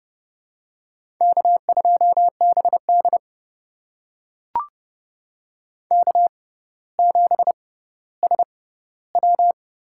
Morse code, heard as a keyed tone, about 30 wpm, spaced as for 15 wpm, 700 Hz. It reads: K2BD K7SW
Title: USA Call Signs - 6x Character Spacing - 30wpm